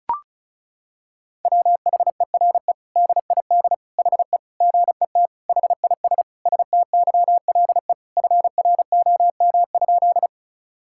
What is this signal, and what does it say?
Morse code, heard as a keyed tone, about 36 wpm, 700 Hz. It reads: WHERE DID HE GET HIS STYLE FROM?
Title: Sentences from Top 1000 Words - Review Part 5 - 35wpm